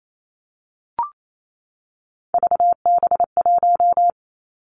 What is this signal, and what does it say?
Morse code, heard as a keyed tone, about 28 wpm, 700 Hz. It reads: VB1
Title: Three Challenging Letter-Number Combinations 28wpm